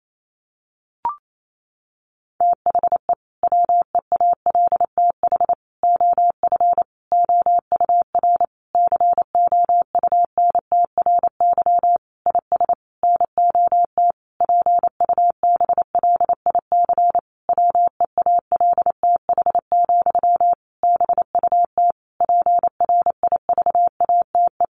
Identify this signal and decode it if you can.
Morse code, heard as a keyed tone, about 28 wpm, 700 Hz. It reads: THE WEALTH OF OUR COUNTRY IS NOT PUBLIC WEALTH, BUT PRIVATE
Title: Sentences from Top 1200 Words 28wpm